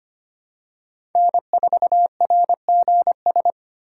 Morse code, heard as a keyed tone, about 25 wpm, 700 Hz. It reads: N4RGS
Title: USA Call Signs - 3x Character Spacing - 25wpm